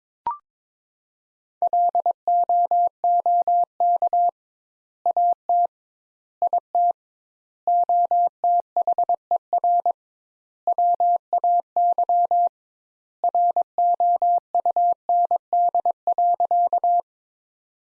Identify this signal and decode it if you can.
Morse code, heard as a keyed tone, about 22 wpm, 700 Hz. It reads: LOOK AT IT OTHER WAY ROUND.
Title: Sentences Top 200 Words - 22wpm